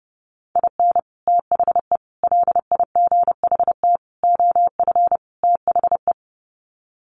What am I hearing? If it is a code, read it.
Morse code, heard as a keyed tone, about 30 wpm, 700 Hz. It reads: IN THE LIGHT OF THE
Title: Sets of 5 Words Encore2 30wpm